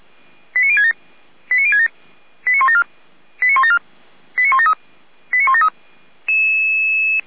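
Five-tone selective call (ZVEI), heard as quick runs of five tones, tones of about 70 ms. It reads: 89067 89067 89174 89174 89173 89173
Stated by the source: ZVEI five-tone selective call received on SDR.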